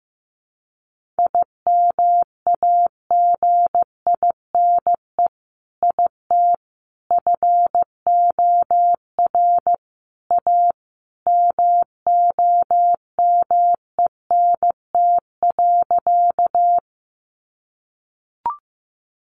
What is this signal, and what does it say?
Morse code, heard as a keyed tone, about 15 wpm, 700 Hz. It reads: IMAGINE IT FOR A MOMENT.
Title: Sentences from Top 1400 Words 15wpm